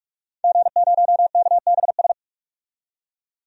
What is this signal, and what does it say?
Morse code, heard as a keyed tone, about 46 wpm, 700 Hz. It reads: G0KBS
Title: DXCC-100 Call Signs - 5 Characters - 45wpm